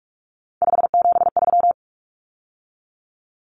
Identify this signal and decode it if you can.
Morse code, heard as a keyed tone, about 44 wpm, 700 Hz. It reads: 573
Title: Numbers - 3-Digit 45wpm